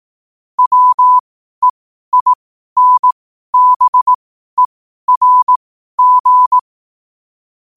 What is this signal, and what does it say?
Morse code, heard as a keyed tone, about 18 wpm, spaced as for 12 wpm, 1000 Hz. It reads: WEINBERG
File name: qotdfiles / 12wpm